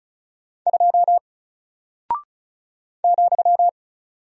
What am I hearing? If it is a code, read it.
Morse code, heard as a keyed tone, about 36 wpm, 700 Hz. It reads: 2 ,
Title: Single Character (Letter:Number:Punctuation:Prosign) Alt 35wpm